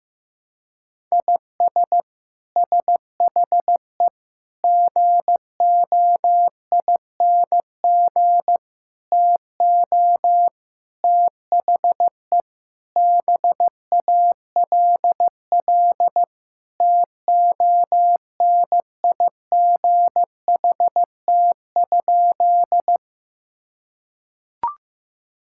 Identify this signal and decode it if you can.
Morse code, heard as a keyed tone, about 15 wpm, 700 Hz. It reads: IS SHE GOING TO THE BALL TONIGHT?
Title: Sentences from Top 1000 Words - Review Part 6 - 15wpm